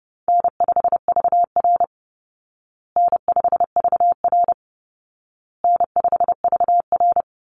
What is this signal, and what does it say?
Morse code, heard as a keyed tone, about 30 wpm, 700 Hz. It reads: N5VR N5VR N5VR
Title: Easy Call Signs Repeated 3x - 30wpm